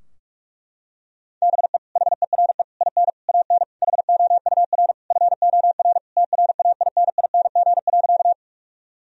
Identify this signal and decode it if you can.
Morse code, heard as a keyed tone, about 46 wpm, 700 Hz. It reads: BE HERE IN AN HOUR FOR TRAINING.
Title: Sentences from Top 1000 Words - Review Part 4 - 45wpm